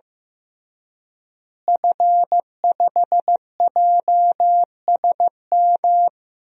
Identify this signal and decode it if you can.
Morse code, heard as a keyed tone, about 15 wpm, 700 Hz. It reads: F5JSM
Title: DXCC-100 Call Signs - 5 Characters - Repeated 3x - 15wpm